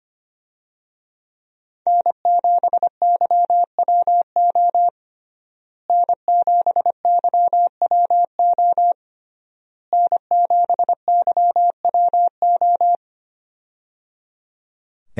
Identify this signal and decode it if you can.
Morse code, heard as a keyed tone, about 25 wpm, 700 Hz. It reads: N7YWO N7YWO N7YWO
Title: USA Call Signs - 5 Characters - Repeated 3x - 25wpm